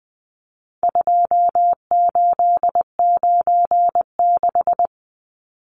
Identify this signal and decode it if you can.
Morse code, heard as a keyed tone, about 20 wpm, 700 Hz. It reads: 2896
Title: Numbers - 4-Digit 20wpm